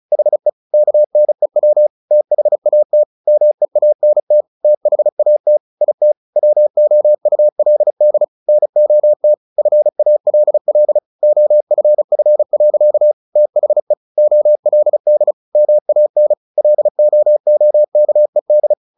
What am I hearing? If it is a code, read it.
Morse code, heard as a keyed tone, about 34 wpm, 600 Hz. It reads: HE KNEW THAT MEANT THAT IT WOULD NOT FALL OFF. THE OLD MAN LOOKED
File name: the old man and the sea 2 35wpm